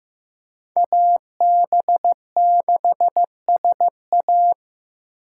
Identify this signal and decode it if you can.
Morse code, heard as a keyed tone, about 15 wpm, 700 Hz. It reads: AB6SA
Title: USA Call Signs - 8x Character Spacing - 15wpm